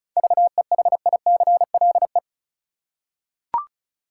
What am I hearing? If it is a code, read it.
Morse code, heard as a keyed tone, about 36 wpm, 700 Hz. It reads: VEHICLE
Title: Seven Letter Words - 35wpm